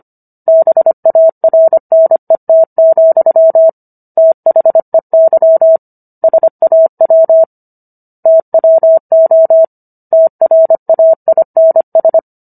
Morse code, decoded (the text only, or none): BARNET, THEY SAW TWO TRAINS